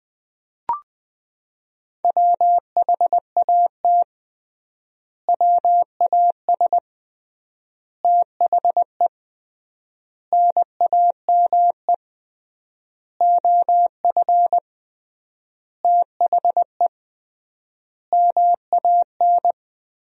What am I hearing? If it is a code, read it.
Morse code, heard as a keyed tone, about 21 wpm, 700 Hz. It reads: WHAT WAS THE NAME OF THE MAN
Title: 3x Word-Spacing - Sentences from Top 300 Words - 20wpm